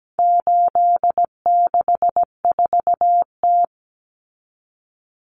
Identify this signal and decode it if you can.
Morse code, heard as a keyed tone, about 17 wpm, 700 Hz. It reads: 864T
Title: Common Cut Numbers - 4-Digit 17wpm